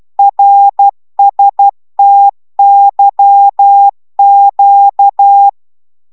Morse code, heard as a keyed tone, about 12 wpm, 800 Hz. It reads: RSTYQ